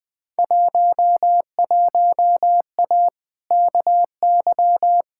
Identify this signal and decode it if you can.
Morse code, heard as a keyed tone, about 20 wpm, 700 Hz. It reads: 11A KY
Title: Field Day Exchanges Encore - 20wpm